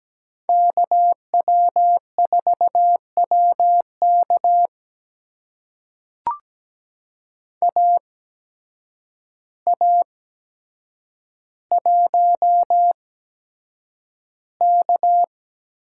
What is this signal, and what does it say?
Morse code, heard as a keyed tone, about 17 wpm, spaced as for 8 wpm, 700 Hz. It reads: KW4WK AA1K
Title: USA Call Signs - 5 Characters - 8x Character Spacing - 17wpm